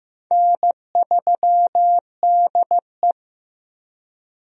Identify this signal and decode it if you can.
Morse code, heard as a keyed tone, about 15 wpm, 700 Hz. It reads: N3DE
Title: USA Call Signs - 4 Characters - Repeated 3x - 15wpm